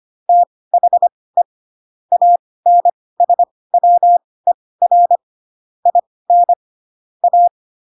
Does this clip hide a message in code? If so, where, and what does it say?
Yes: Morse code from 0.3 to 7.5 s, heard as a keyed tone, about 25 wpm, spaced as for 18 wpm, 700 Hz. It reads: THE ANSWER IN A